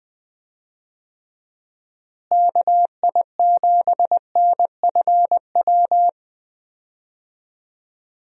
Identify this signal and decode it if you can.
Morse code, heard as a keyed tone, about 20 wpm, 700 Hz. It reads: KI7NFW
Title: USA Call Signs - 6 Characters - Repeated 2x - 20wpm